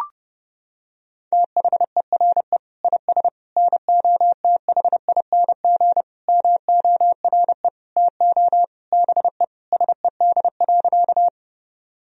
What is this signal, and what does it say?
Morse code, heard as a keyed tone, about 30 wpm, 700 Hz. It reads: THERE IS NOTHING MORE TO BE SED.
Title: Sentences from Top 500 Words and Abbreviations 30wpm